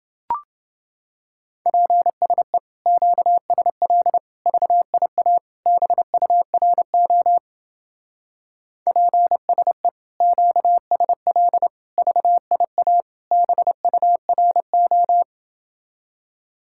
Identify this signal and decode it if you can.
Morse code, heard as a keyed tone, about 29 wpm, 700 Hz. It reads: PSE QSL VIA BURO PSE QSL VIA BURO